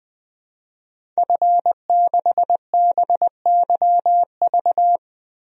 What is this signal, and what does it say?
Morse code, heard as a keyed tone, about 21 wpm, 700 Hz. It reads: F6BYV
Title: DXCC-100 Call Signs - 5 Characters - Repeated 3x - 20wpm